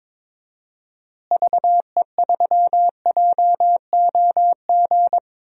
Morse code, heard as a keyed tone, about 21 wpm, 700 Hz. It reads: VE3JOG
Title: DXCC-100 Call Signs - 6 Characters - Repeated 3x - 22wpm